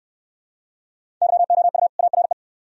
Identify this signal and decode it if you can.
Morse code, heard as a keyed tone, about 69 wpm, 700 Hz. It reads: YOU ARE